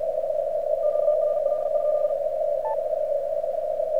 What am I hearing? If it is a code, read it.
Morse code, heard as a keyed tone, about 48 wpm, 1200 Hz, with radio noise. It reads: QRL?
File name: archived / CRSnet